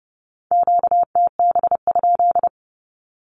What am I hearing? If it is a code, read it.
Morse code, heard as a keyed tone, about 32 wpm, 700 Hz. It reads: QTB?
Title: Q-Codes with Spoken Meaning - 30wpm